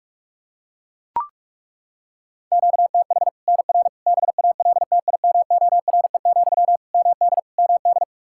Morse code, heard as a keyed tone, about 41 wpm, 700 Hz. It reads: QTH NR BALTIMORE, MD MD
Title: Practice QSO - Exchange 1 - Beginning - 45wpm